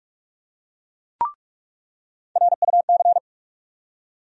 Morse code, heard as a keyed tone, about 46 wpm, 700 Hz. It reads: RUC
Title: USA Call Sign Postfixes - 3 Letters 45wpm